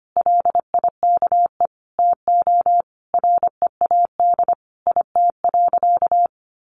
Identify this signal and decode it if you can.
Morse code, heard as a keyed tone, about 25 wpm, 700 Hz. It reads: LIKE TO READ IT.
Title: Sentences from Top 500 Words 25wpm